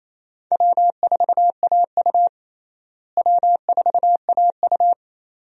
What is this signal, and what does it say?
Morse code, heard as a keyed tone, about 28 wpm, 700 Hz. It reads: W4AU W4AU
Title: USA Call Signs - 4 Characters - Repeated 2x - 28wpm